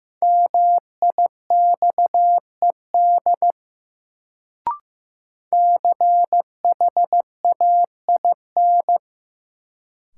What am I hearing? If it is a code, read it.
Morse code, heard as a keyed tone, about 15 wpm, 700 Hz. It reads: MIXED CHAIN
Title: Sentences from Top 1500 Words - New Word Drill - 15wpm